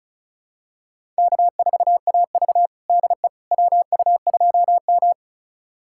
Morse code, heard as a keyed tone, about 36 wpm, 700 Hz. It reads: K4AV DE WU2M
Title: USA Callsign DE Callsign - 4 Characters - 35wpm